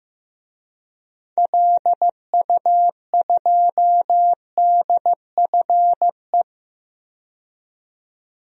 Morse code, heard as a keyed tone, about 16 wpm, 700 Hz. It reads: LU2DFE
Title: DXCC-100 Call Signs - 6 Characters - 15wpm